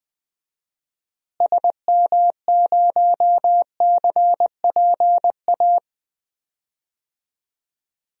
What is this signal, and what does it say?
Morse code, heard as a keyed tone, about 20 wpm, 700 Hz. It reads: SM0CPA
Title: DXCC-100 Call Signs - 6 Characters - Repeated 3x - 20wpm